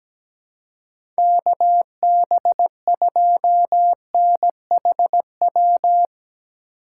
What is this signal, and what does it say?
Morse code, heard as a keyed tone, about 17 wpm, 700 Hz. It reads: KB2NHW
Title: USA Call Signs - 6 Characters - 3x Character Spacing - 17wpm